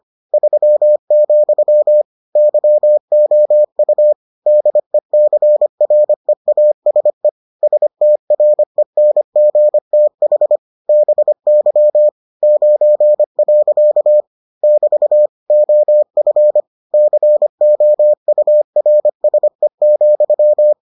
Morse code, heard as a keyed tone, about 25 wpm, 600 Hz. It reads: 3, YOU DECREASE STRENGTH BY 9. <BT> OF COURSE,